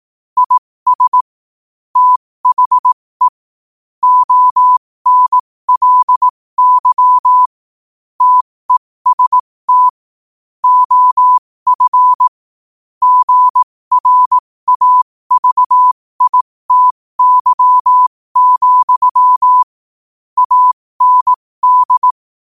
Morse code, heard as a keyed tone, about 18 wpm, spaced as for 15 wpm, 1000 Hz. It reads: IS THE ONLY TEST OF GRAVITY, AND
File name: qotdfiles / 15wpm